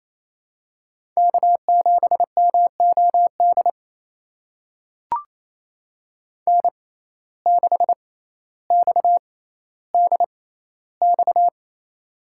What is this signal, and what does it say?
Morse code, heard as a keyed tone, about 28 wpm, spaced as for 14 wpm, 700 Hz. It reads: K7MOD N6XDX
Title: USA Call Signs - 5 Characters - 6x Character Spacing - 28wpm